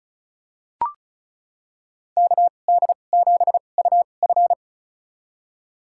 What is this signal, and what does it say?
Morse code, heard as a keyed tone, about 34 wpm, spaced as for 25 wpm, 700 Hz. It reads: KD7UF
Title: USA Call Signs - 2x Character Spacing - 35wpm